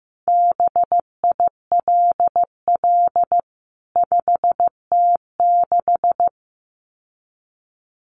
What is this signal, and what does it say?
Morse code, heard as a keyed tone, about 15 wpm, 700 Hz. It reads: BILL 5T6
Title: CWT Exchange 15wpm